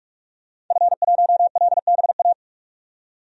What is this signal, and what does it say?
Morse code, heard as a keyed tone, about 46 wpm, 700 Hz. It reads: F1LBA
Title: DXCC-100 Call Signs - 5 Characters - 45wpm